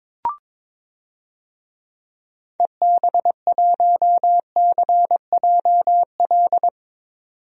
Morse code, heard as a keyed tone, about 22 wpm, 700 Hz. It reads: EB1CJL
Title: DXCC-100 Call Signs - 6 Characters - 22wpm